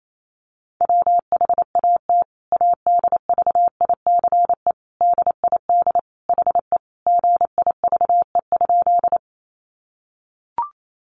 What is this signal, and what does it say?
Morse code, heard as a keyed tone, about 28 wpm, 700 Hz. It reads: WHAT ADVICE DID HE GIVE?
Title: Sentences from Top 1100 Words 28wpm